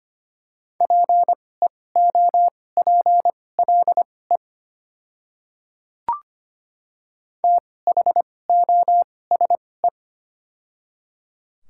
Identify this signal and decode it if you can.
Morse code, heard as a keyed tone, about 25 wpm, spaced as for 18 wpm, 700 Hz. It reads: PEOPLE THOSE
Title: 2x Character Spacing - Top 300 Words - 25wpm